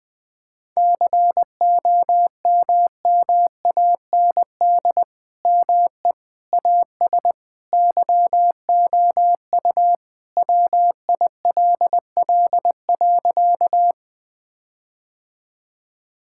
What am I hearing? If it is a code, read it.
Morse code, heard as a keyed tone, about 20 wpm, 700 Hz. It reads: COMMAND ME AS YOU WILL.
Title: Sentences from Top 1600 Words 20wpm